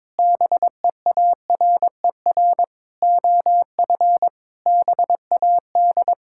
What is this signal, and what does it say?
Morse code, heard as a keyed tone, about 22 wpm, 700 Hz. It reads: BEARER OF BAD
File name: Sets of 9 Words - 22wpm